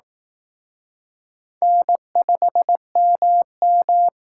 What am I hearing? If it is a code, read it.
Morse code, heard as a keyed tone, about 18 wpm, 700 Hz. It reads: N5MM
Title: Speed Racing - USA 1x2 Call Signs 15wpm